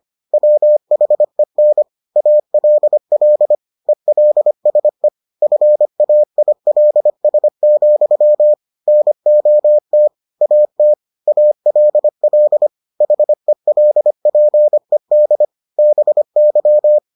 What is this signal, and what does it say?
Morse code, heard as a keyed tone, about 25 wpm, 600 Hz. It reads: WHEN ALL ELSE FAILS, NOT AT ALL HELPED BY